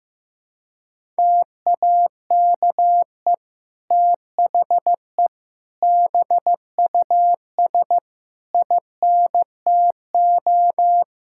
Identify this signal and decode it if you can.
Morse code, heard as a keyed tone, about 15 wpm, 700 Hz. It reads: TAKE THE BUS INTO
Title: Sentences from Top 1000 Words - Review Part 6 - 15wpm